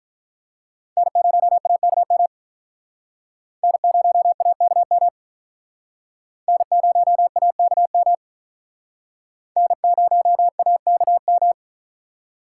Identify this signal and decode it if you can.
Morse code, heard as a keyed tone, about 46 wpm, 700 Hz. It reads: N0AKM N0AKM N0AKM N0AKM